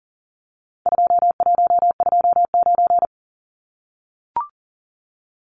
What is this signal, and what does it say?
Morse code, heard as a keyed tone, about 40 wpm, 700 Hz. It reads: 2129
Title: Numbers - 4-Digit 40wpm